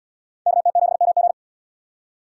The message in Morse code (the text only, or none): KT7MG